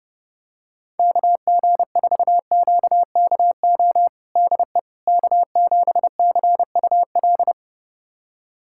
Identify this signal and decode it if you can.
Morse code, heard as a keyed tone, about 30 wpm, 700 Hz. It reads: KG4QKO DE K7CUL
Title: USA Callsign DE Callsign 30wpm